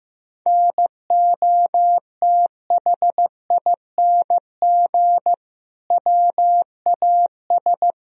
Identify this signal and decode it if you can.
Morse code, heard as a keyed tone, about 15 wpm, 700 Hz. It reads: NOTHING WAS
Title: Sentences from Top 1000 Words - Review Part 2 - 15wpm